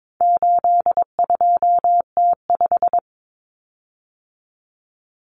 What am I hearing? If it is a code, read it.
Morse code, heard as a keyed tone, about 22 wpm, 700 Hz. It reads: 82T5
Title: Common Cut Numbers - 4-Digit 22wpm